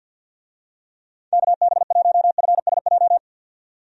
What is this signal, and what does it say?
Morse code, heard as a keyed tone, about 51 wpm, 700 Hz. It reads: KB1FSJ
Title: USA Call Signs - 6 Characters - 50wpm